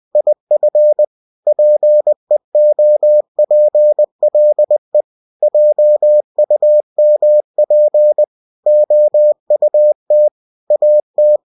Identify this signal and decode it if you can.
Morse code, heard as a keyed tone, about 20 wpm, 600 Hz. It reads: IF PEOPLE JUMP OUT AT
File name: cw-books / winnie-the-pooh / winnie-20wpm-page-0082